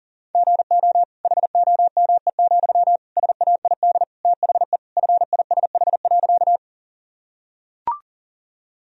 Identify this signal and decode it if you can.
Morse code, heard as a keyed tone, about 37 wpm, 700 Hz. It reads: GO HOME, SAID THE FISH.
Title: Sentences from Top 1000 Words - Review Part 1 - 40wpm